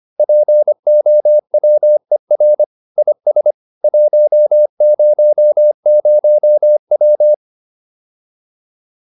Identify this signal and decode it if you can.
Morse code, heard as a keyed tone, about 25 wpm, 600 Hz. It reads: POWER IS 100W